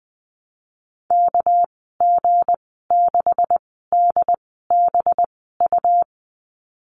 Morse code, heard as a keyed tone, about 20 wpm, spaced as for 15 wpm, 700 Hz. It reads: KG6DBU
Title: 200 Lesson 32g - Letter k - Call Signs 20wpm